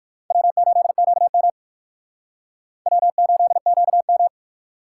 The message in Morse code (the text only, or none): W8QM W8QM